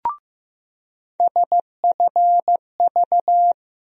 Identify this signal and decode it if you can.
Morse code, heard as a keyed tone, about 15 wpm, 700 Hz. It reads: SFV
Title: USA Call Sign Postfixes - 3 Letters 15wpm